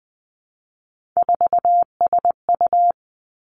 Morse code, heard as a keyed tone, about 19 wpm, 700 Hz. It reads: 4SU
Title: Three Challenging Letter-Number Combinations 20wpm